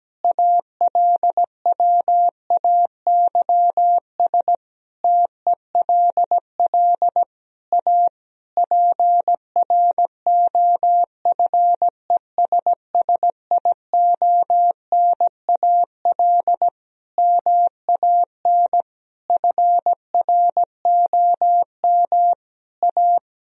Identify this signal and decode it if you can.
Morse code, heard as a keyed tone, about 17 wpm, 700 Hz. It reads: ALWAYS TELL A PROFESSIONAL MAN FROM A
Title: Sentences from Top 1000 Words - Review Part 4 - 17wpm